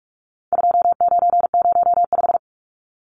Morse code, heard as a keyed tone, about 44 wpm, 700 Hz. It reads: 2905
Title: Numbers - 4-Digit 45wpm